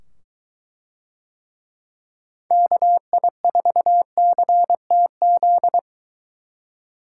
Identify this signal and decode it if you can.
Morse code, heard as a keyed tone, about 22 wpm, 700 Hz. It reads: KI4CTZ